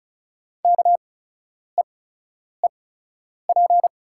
Morse code, heard as a keyed tone, about 34 wpm, spaced as for 10 wpm, 700 Hz. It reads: KEEP
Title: 8x Character Spacing - Top 300 Words - 35wpm